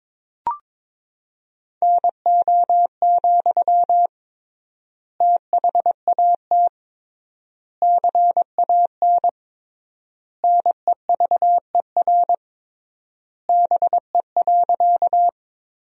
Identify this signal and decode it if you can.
Morse code, heard as a keyed tone, about 22 wpm, 700 Hz. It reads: NO, THAT CAN NEVER BE.
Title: 3x Word-Spacing - Sentences from Top 300 Words - 22wpm